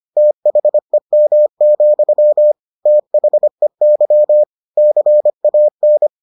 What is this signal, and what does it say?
Morse code, heard as a keyed tone, about 25 wpm, 600 Hz. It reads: THEM, THEY CAN